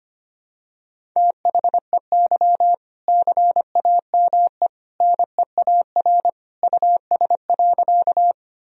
Morse code, heard as a keyed tone, about 25 wpm, 700 Hz. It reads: THEY CAME NEAR US.
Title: Sentences from Top 1000 Words - Review Part 1 - 25wpm